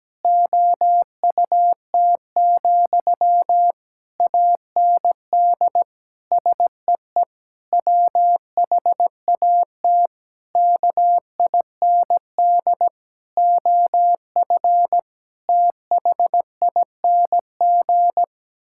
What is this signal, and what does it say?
Morse code, heard as a keyed tone, about 17 wpm, 700 Hz. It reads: OUT, AND SEE WHAT KIND OF THING
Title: Sentences from Top 1000 Words - Review Part 6 - 17wpm